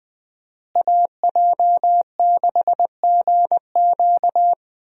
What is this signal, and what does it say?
Morse code, heard as a keyed tone, about 20 wpm, 700 Hz. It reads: AJ6GQ
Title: USA Call Signs - 5 Characters - Repeated 3x - 20wpm